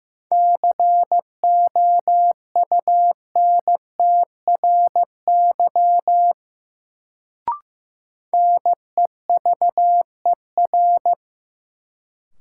Morse code, heard as a keyed tone, about 15 wpm, 700 Hz. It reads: COUNTRY NEVER
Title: Sentences from Top 300 Words - New Word Drill - 15wpm